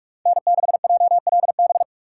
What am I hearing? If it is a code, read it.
Morse code, heard as a keyed tone, about 40 wpm, 700 Hz. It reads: N6JLB